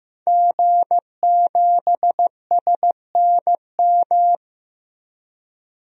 Morse code, heard as a keyed tone, about 15 wpm, 700 Hz. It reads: G7SNM
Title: DXCC-100 Call Signs - 5 Characters - Repeated 2x - 15wpm